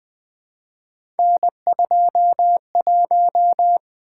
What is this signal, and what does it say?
Morse code, heard as a keyed tone, about 20 wpm, 700 Hz. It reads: N21
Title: Three Letter-Number Combinations - 20wpm